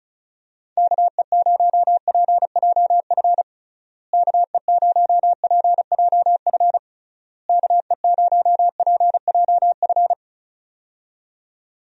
Morse code, heard as a keyed tone, about 36 wpm, 700 Hz. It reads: KE0PJF KE0PJF KE0PJF
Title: USA Call Signs - 6 Characters - Repeated 3x - 35wpm